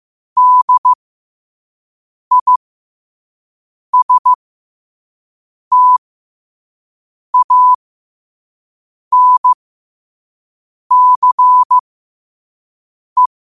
Morse code, heard as a keyed tone, about 15 wpm, spaced as for 5 wpm, 1000 Hz. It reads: DISTANCE